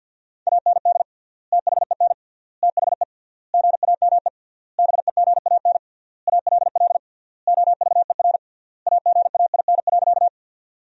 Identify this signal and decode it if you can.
Morse code, heard as a keyed tone, about 50 wpm, 700 Hz. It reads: AND THEN THE GAME BEGAN ALL OVER AGAIN.
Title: 3x Word-Spacing - Sentences from Top 500 Words - 50wpm